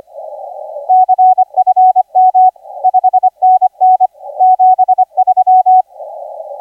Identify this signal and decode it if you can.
Morse code, heard as a keyed tone, about 25 wpm, 740 Hz, with radio noise. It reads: CFM 5NN 73
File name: ZA DL7UCX 1420UTC 20m